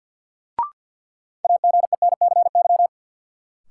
Morse code, heard as a keyed tone, about 52 wpm, 700 Hz. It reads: AGENCY
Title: Sentences from Top 1300 Words - New Word Drill - 50wpm